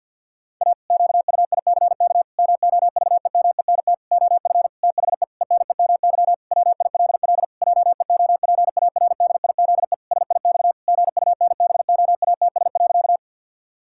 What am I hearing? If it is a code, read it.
Morse code, heard as a keyed tone, about 50 wpm, 700 Hz. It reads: A QUICK MOVEMENT OF THE ENEMY WILL JEOPARDIZE SIX GUNBOATS.
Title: Pangram Challenge v2 - 50wpm